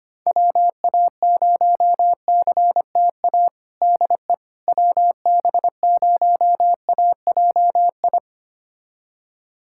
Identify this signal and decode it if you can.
Morse code, heard as a keyed tone, about 25 wpm, 700 Hz. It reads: WA0CTA DE WB0AJI
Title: USA Callsign DE Callsign - 6 Characters - 25wpm